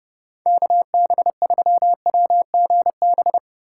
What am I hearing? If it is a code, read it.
Morse code, heard as a keyed tone, about 32 wpm, 700 Hz. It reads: KB3WGB
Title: USA Call Signs - 3x Character Spacing - 30wpm